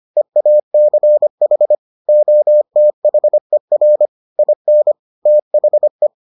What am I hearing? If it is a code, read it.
Morse code, heard as a keyed tone, about 25 wpm, 600 Hz. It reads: EACH OTHER IN THE